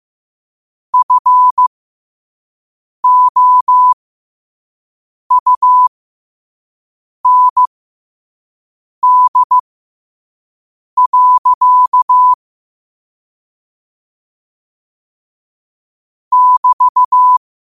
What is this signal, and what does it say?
Morse code, heard as a keyed tone, about 15 wpm, spaced as for 5 wpm, 1000 Hz. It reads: FOUND. <BT>
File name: qotdfiles / 5wpm